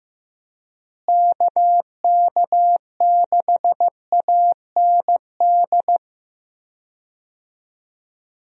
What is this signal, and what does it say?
Morse code, heard as a keyed tone, about 15 wpm, 700 Hz. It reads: KK6AND
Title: USA Call Signs - 6 Characters - Repeated 2x - 15wpm